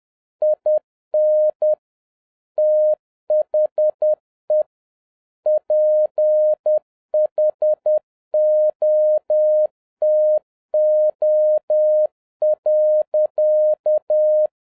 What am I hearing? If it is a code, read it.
Morse code, heard as a keyed tone, about 10 wpm, 610 Hz. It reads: IN THE PHOTO.
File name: 2011-07-mp3 / n41 EXCITING QRP CW OPERATION 10wpm